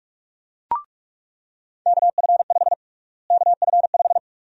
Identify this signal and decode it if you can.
Morse code, heard as a keyed tone, about 46 wpm, 700 Hz. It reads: KF5 KF5
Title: USA Call Sign Prefixes - Repeated 2x - 45wpm